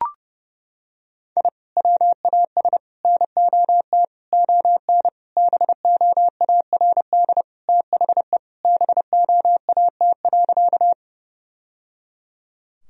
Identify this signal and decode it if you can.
Morse code, heard as a keyed tone, about 30 wpm, 700 Hz. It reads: I WAS NOT ON BOARD THE BOAT.
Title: Sentences from Top 1000 Words - Review Part 4 - 30wpm